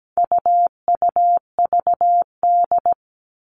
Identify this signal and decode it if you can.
Morse code, heard as a keyed tone, about 17 wpm, 700 Hz. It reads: UUVD
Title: All Cut Numbers - 4-Digit 17wpm